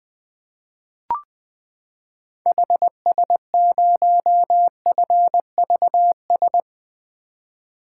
Morse code, heard as a keyed tone, about 20 wpm, 700 Hz. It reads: HS0FVS
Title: 2023 POTA Activator Call Signs - 20wpm